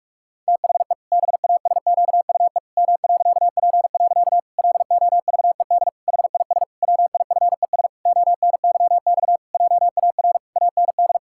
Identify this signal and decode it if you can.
Morse code, heard as a keyed tone, about 46 wpm, 700 Hz. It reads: THE BASQUE M.P. LOVED HIS WIFES ONYX JAR AND
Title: Pangram Challenge v2 - 45wpm